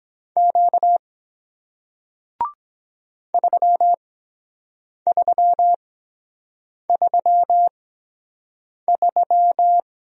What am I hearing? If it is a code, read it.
Morse code, heard as a keyed tone, about 23 wpm, spaced as for 7 wpm, 700 Hz. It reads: Q 3333